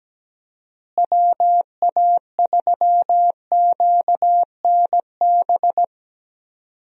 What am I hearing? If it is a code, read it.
Morse code, heard as a keyed tone, about 17 wpm, 700 Hz. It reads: WA3QNB
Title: USA Call Signs - 6 Characters - Repeated 2x - 17wpm